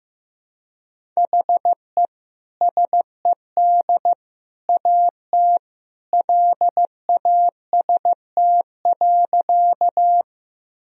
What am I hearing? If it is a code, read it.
Morse code, heard as a keyed tone, about 15 wpm, 700 Hz. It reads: HE SED AT LAST.
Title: Sentences from Top 300 Words and Abbreviations 15wpm